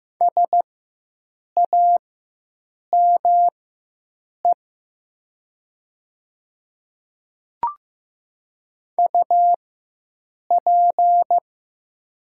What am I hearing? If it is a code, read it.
Morse code, heard as a keyed tone, about 15 wpm, spaced as for 7 wpm, 700 Hz. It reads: SAME UP
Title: 4x Character Spacing - Top 200 Words - 15wpm